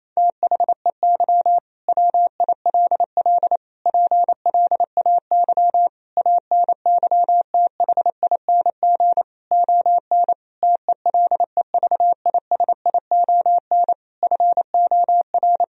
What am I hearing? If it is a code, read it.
Morse code, heard as a keyed tone, about 27 wpm, 700 Hz. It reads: THEY WILL PLAY ANYTHING ON TELEVISION FOR
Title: Sentences from Top 1000 Words - Review Part 4 - 28wpm